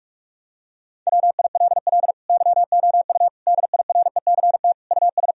Morse code, heard as a keyed tone, about 46 wpm, 700 Hz. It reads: WILL YOU DIRECT US
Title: Sentences from Top 1000 Words - Review Part 2 - 45wpm